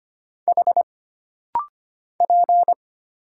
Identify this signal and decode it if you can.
Morse code, heard as a keyed tone, about 25 wpm, 700 Hz. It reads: H P
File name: Single Character (Letter:Number:Punctuation:Prosign) - Rapid-Fire - 25wpm